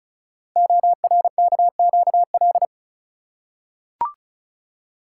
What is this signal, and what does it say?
Morse code, heard as a keyed tone, about 36 wpm, 700 Hz. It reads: ORKQL
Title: Code Groups - 5 Letters - 35wpm